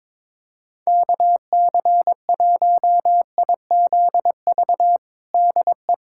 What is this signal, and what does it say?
Morse code, heard as a keyed tone, about 22 wpm, 700 Hz. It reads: KC1IZV DE
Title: USA Callsign DE Callsign 22wpm